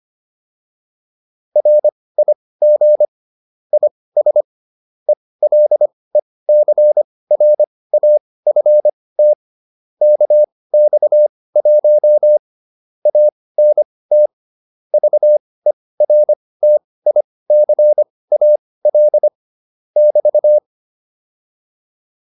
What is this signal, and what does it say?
Morse code, heard as a keyed tone, about 25 wpm, spaced as for 18 wpm, 600 Hz. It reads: RIG IS ELECRAFT KX1 ANT VERTICAL <BT>
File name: QSO207 18